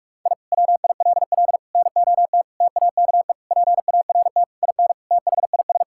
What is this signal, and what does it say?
Morse code, heard as a keyed tone, about 41 wpm, 700 Hz. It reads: I WILL NOT TAKE PART IN THIS
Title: Sentences from Top 1000 Words - Review Part 6 - 45wpm